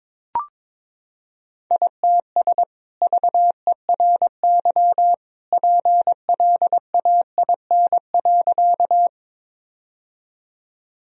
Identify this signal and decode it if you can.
Morse code, heard as a keyed tone, about 22 wpm, 700 Hz. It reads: ITS VERY PLAIN.
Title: Sentences from Top 1000 Words - Review Part 2 - 22wpm